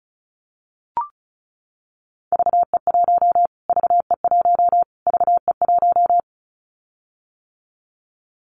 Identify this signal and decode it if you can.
Morse code, heard as a keyed tone, about 34 wpm, 700 Hz. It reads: VE1 VE1 VE1
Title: Callsign Prefixes for Top 25 DXCC - Repeated 3x - 35wpm